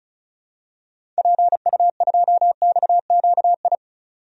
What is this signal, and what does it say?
Morse code, heard as a keyed tone, about 34 wpm, 700 Hz. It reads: PU2XQI
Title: DXCC-100 Call Signs - 6 Characters - 35wpm